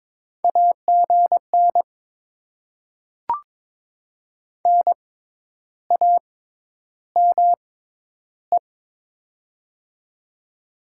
Morse code, heard as a keyed tone, about 22 wpm, spaced as for 10 wpm, 700 Hz. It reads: AGN NAME
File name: Top 20 QSO Elements - 6x Character Spacing - 22wpm